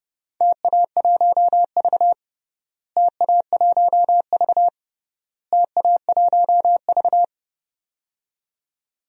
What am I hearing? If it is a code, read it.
Morse code, heard as a keyed tone, about 29 wpm, 700 Hz. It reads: TA1V TA1V TA1V
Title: DXCC-100 Call Signs - 4 Characters - Repeated 3x - 30wpm